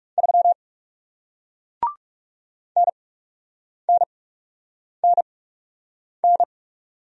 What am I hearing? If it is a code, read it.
Morse code, heard as a keyed tone, about 41 wpm, spaced as for 8 wpm, 700 Hz. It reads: 3 NNNN